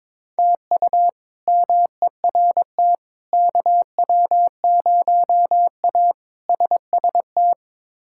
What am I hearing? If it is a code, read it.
Morse code, heard as a keyed tone, about 22 wpm, 700 Hz. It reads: TU MERT KW0A SST
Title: K1USN SST QSO Practice - 22wpm